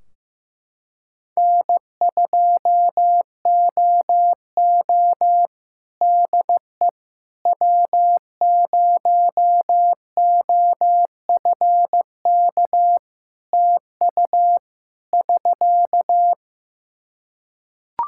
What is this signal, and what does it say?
Morse code, heard as a keyed tone, about 15 wpm, 700 Hz. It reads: N2OO DE W0OFK TU <SK>
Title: K1USN SST QSO Practice 15wpm